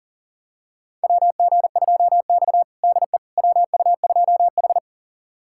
Morse code, heard as a keyed tone, about 40 wpm, 700 Hz. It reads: WG2X DE WU2H
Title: USA Callsign DE Callsign - 4 Characters - 40wpm